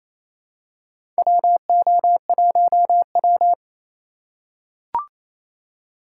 Morse code, heard as a keyed tone, about 28 wpm, 700 Hz. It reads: WO1W
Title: USA Call Signs - 4 Characters - 3x Character Spacing - 28wpm